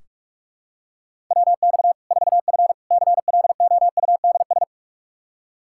Morse code, heard as a keyed tone, about 46 wpm, 700 Hz. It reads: WX 4F CLOUDS
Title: Practice QSO - Exchange 2 - Info - 45wpm